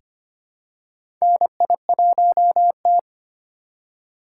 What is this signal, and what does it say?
Morse code, heard as a keyed tone, about 25 wpm, 700 Hz. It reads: NI1T
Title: USA Call Signs - 6x Character Spacing - 25wpm